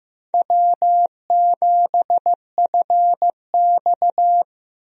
Morse code, heard as a keyed tone, about 15 wpm, 700 Hz. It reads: W7FX
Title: USA Call Signs - 4 Characters - Repeated 3x - 15wpm